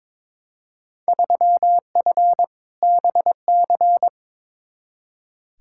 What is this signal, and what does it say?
Morse code, heard as a keyed tone, about 21 wpm, 700 Hz. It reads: 3F BC
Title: Field Day Exchanges Encore - 22wpm